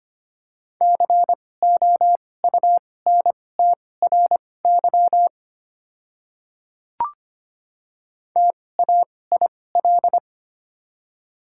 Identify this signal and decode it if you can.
Morse code, heard as a keyed tone, about 25 wpm, spaced as for 18 wpm, 700 Hz. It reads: COUNTRY TAIL
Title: 2x Character Spacing - Top 500 Words - 25wpm